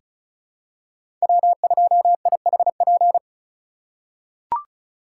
Morse code, heard as a keyed tone, about 36 wpm, 700 Hz. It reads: W2IHP
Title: USA Call Signs - 2x Character Spacing - 35wpm